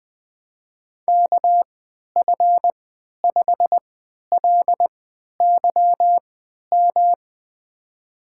Morse code, heard as a keyed tone, about 21 wpm, spaced as for 11 wpm, 700 Hz. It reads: KF5LYM
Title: USA Call Signs - 6 Characters - 3x Character Spacing - 20wpm